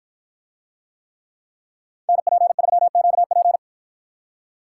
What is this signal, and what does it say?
Morse code, heard as a keyed tone, about 53 wpm, 700 Hz. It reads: NP3QP